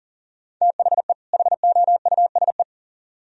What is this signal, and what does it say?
Morse code, heard as a keyed tone, about 49 wpm, 700 Hz. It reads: THE HOUSE
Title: Speed-Racing - Sets of 2 Words 40wpm